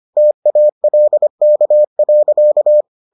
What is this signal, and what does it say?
Morse code, heard as a keyed tone, about 25 wpm, 600 Hz. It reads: TALK.